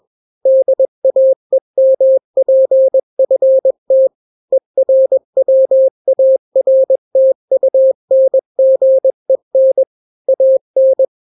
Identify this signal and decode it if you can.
Morse code, heard as a keyed tone, about 21 wpm, 530 Hz. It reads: DAEMPFT ERWARTUNGEN AN
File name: daily / 20wpm